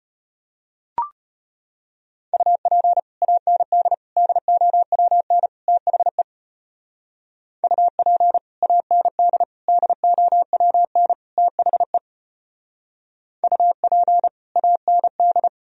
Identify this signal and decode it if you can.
Morse code, heard as a keyed tone, about 31 wpm, 700 Hz. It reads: UP AND DOWN THE UP AND DOWN THE UP AND
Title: Speed-Racing - Sets of 4 Words 25wpm